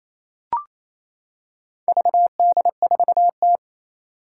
Morse code, heard as a keyed tone, about 27 wpm, 700 Hz. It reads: VD4T
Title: Four Challenging Letter-Number Combinations 28wpm